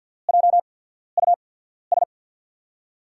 Morse code, heard as a keyed tone, about 48 wpm, spaced as for 14 wpm, 700 Hz. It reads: JUS
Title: USA Call Signs - 6 Characters - 8x Character Spacing - 50wpm